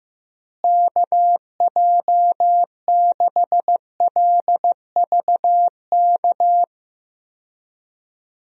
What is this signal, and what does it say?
Morse code, heard as a keyed tone, about 15 wpm, 700 Hz. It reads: KJ6LVK
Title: USA Call Signs - 6 Characters - Repeated 3x - 15wpm